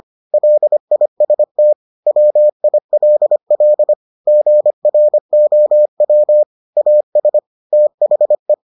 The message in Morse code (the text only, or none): LIST WILL GROW AS THE